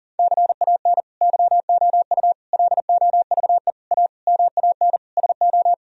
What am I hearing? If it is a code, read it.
Morse code, heard as a keyed tone, about 40 wpm, 700 Hz. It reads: CAN YOU LOVE A MAN SO
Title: Sentences from Top 1000 Words - Review Part 1 - 40wpm